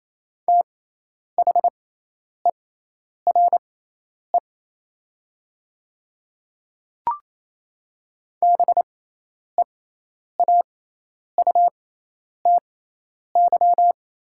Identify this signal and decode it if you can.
Morse code, heard as a keyed tone, about 28 wpm, spaced as for 10 wpm, 700 Hz. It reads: THERE BEAUTY
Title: 6x Character Spacing - Top 500 Words - 28wpm